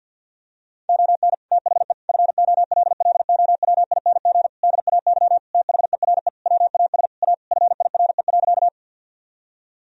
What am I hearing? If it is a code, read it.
Morse code, heard as a keyed tone, about 50 wpm, 700 Hz. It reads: ON THE FOLLOWING DAY THERE WAS A FIRE.
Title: Sentences from Top 1000 Words - Review Part 4 - 50wpm